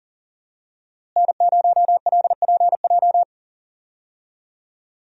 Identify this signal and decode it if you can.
Morse code, heard as a keyed tone, about 40 wpm, 700 Hz. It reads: N0LPJ
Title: USA Call Signs - 5 Characters - Repeated 2x - 40wpm